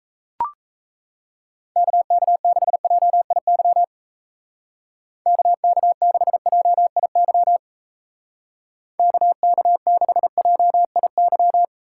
Morse code, heard as a keyed tone, about 32 wpm, 700 Hz. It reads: KK6JIY KK6JIY KK6JIY
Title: Speed Racing - USA 2x3 Call Signs 28wpm